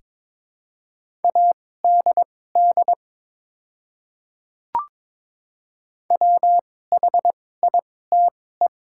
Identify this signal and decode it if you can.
Morse code, heard as a keyed tone, about 21 wpm, spaced as for 16 wpm, 690 Hz. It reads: ADD WHITE
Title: 2x Character Spacing - Top 500 Words - 22wpm